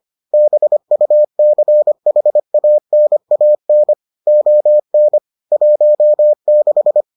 Morse code, heard as a keyed tone, about 25 wpm, 600 Hz. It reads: BUCHANAN ON 16